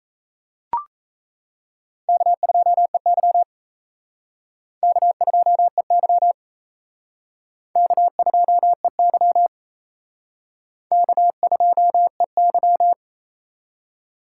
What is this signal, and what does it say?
Morse code, heard as a keyed tone, about 35 wpm, 700 Hz. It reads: K2EY K2EY K2EY K2EY
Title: Speed Racing - USA 1x2 Call Signs 28wpm